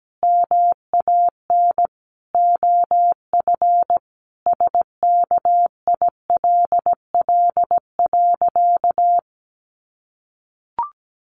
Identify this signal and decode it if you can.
Morse code, heard as a keyed tone, about 17 wpm, 700 Hz. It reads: MAN OF SKILL.
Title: Sentences from Top 1200 Words 17wpm